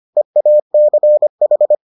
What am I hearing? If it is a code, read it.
Morse code, heard as a keyed tone, about 25 wpm, 600 Hz. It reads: EACH